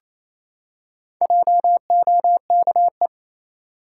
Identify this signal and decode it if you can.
Morse code, heard as a keyed tone, about 28 wpm, 700 Hz. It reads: JOKE
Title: Four Letter Words - 28wpm